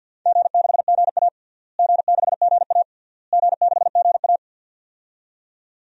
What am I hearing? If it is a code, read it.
Morse code, heard as a keyed tone, about 52 wpm, 700 Hz. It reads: G6GA G6GA G6GA
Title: DXCC-100 Call Signs - 4 Characters - Repeated 3x - 50wpm